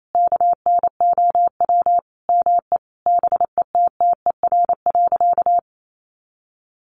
Morse code, heard as a keyed tone, about 28 wpm, 700 Hz. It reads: KNOW ME BETTER.
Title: Sentences from Top 400 Words 28wpm